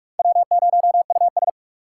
Morse code, heard as a keyed tone, about 44 wpm, 700 Hz. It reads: W0US